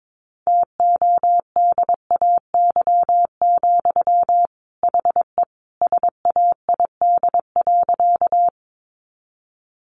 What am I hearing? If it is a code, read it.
Morse code, heard as a keyed tone, about 22 wpm, 700 Hz. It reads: TODAY, HE SAID.
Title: Sentences from Top 600 Words 22wpm